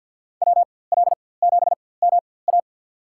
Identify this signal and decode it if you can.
Morse code, heard as a keyed tone, about 46 wpm, spaced as for 23 wpm, 700 Hz. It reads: WL7MA
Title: USA Call Signs - 5 Characters - 4x Character Spacing - 50wpm